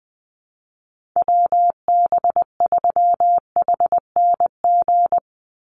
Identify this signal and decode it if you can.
Morse code, heard as a keyed tone, about 20 wpm, 700 Hz. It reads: WB3HNG
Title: USA Call Signs 20wpm